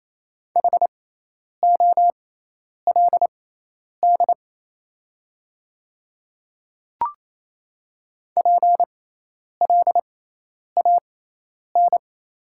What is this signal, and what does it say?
Morse code, heard as a keyed tone, about 28 wpm, spaced as for 10 wpm, 700 Hz. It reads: HOLD PLAN
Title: 6x Character Spacing - Top 500 Words - 28wpm